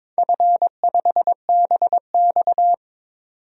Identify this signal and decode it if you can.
Morse code, heard as a keyed tone, about 22 wpm, 700 Hz. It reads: F5BX
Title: DXCC-100 Call Signs - 4 Characters - 22wpm